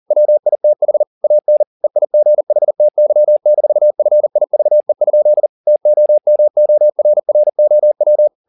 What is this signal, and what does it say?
Morse code, heard as a keyed tone, about 40 wpm, 600 Hz. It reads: WITH AN EIGHTY-FIVE? TOMORROW